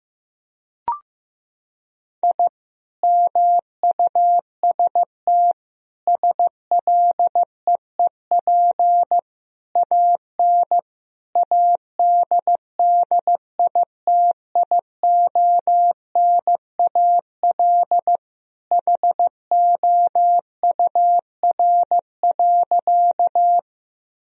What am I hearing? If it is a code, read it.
Morse code, heard as a keyed tone, about 15 wpm, 700 Hz. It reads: I MUST SLEEP AN ADDITIONAL HOUR.
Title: Sentences from Top 1000 Words - Review Part 4 - 15wpm